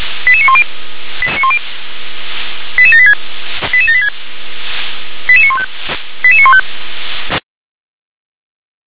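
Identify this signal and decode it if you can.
Five-tone selective call (ZVEI), heard as quick runs of five tones, tones of about 80 ms. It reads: 88910 88786 88786 88915 88915
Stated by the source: ZVEI five-tone selective call received by SDR.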